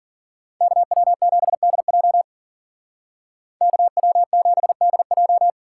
Morse code, heard as a keyed tone, about 52 wpm, 700 Hz. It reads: KW7DJ KW7DJ